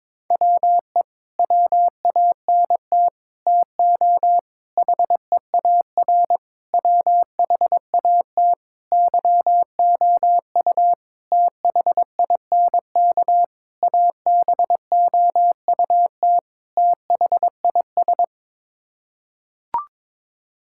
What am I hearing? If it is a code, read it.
Morse code, heard as a keyed tone, about 21 wpm, 700 Hz. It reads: WE WANT TO HEAR WHAT YOU THINK ABOUT THIS
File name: Sets of 9 Words - 22wpm